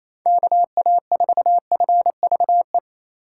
Morse code, heard as a keyed tone, about 28 wpm, 700 Hz. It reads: KA4FVE
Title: USA Call Signs - 6 Characters - 4x Character Spacing - 28wpm